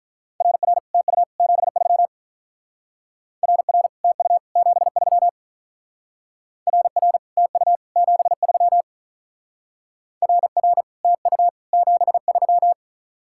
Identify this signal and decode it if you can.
Morse code, heard as a keyed tone, about 43 wpm, 700 Hz. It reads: RR TU 73 RR TU 73 RR TU 73 RR TU 73
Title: Speed-Racing Common QSO Phrases - 35wpm